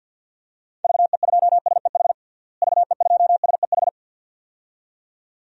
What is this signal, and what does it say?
Morse code, heard as a keyed tone, about 46 wpm, 700 Hz. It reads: VE2SEH VE2SEH
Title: DXCC-100 Call Signs - 6 Characters - Repeated 2x - 50wpm